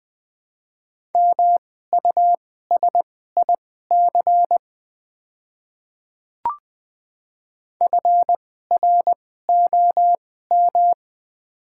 Morse code, heard as a keyed tone, about 20 wpm, spaced as for 14 wpm, 700 Hz. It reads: MUSIC FROM
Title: 2x Character Spacing - Top 300 Words - 20wpm